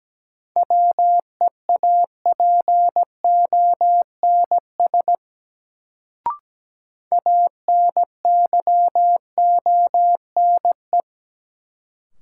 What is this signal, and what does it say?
Morse code, heard as a keyed tone, about 17 wpm, 700 Hz. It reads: WEAPONS ANYONE
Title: Sets of 5 Words Encore - New Word Drill - 17wpm